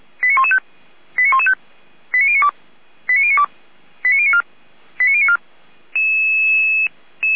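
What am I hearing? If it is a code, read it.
Five-tone selective call (ZVEI), heard as quick runs of five tones, tones of about 70 ms. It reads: 89115 89115 89092 89092 89094 89094
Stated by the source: ZVEI five-tone selective call 89025 received by SDR.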